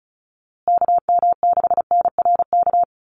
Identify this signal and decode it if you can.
Morse code, heard as a keyed tone, about 34 wpm, 700 Hz. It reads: KM6NRK
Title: USA Call Signs Repeated 2x 35wpm